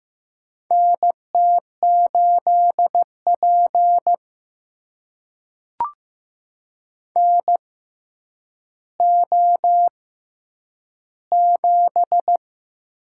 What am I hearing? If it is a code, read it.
Morse code, heard as a keyed tone, about 15 wpm, spaced as for 9 wpm, 700 Hz. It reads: NT8P NO7
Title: USA Call Signs - 4 Characters - 6x Character Spacing - 15wpm